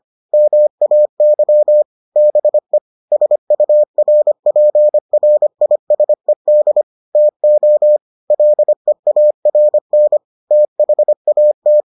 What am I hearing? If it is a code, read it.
Morse code, heard as a keyed tone, about 25 wpm, 600 Hz. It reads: MAY BE SURPRISED TO LEARN THAT